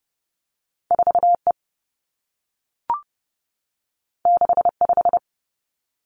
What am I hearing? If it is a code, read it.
Morse code, heard as a keyed tone, about 30 wpm, 700 Hz. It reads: 4 E 6 5
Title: Two Challenging Letter-Number Combinations 30wpm